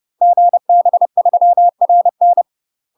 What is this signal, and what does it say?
Morse code, heard as a keyed tone, about 30 wpm, 700 Hz. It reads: GB3RN
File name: UK MASTER QSOs TRAINING / 30WPM / UK MASTER QSOs 0019